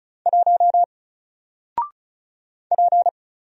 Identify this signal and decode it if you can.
Morse code, heard as a keyed tone, about 34 wpm, 700 Hz. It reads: 1 P
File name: Single-Letter-Number-35wpm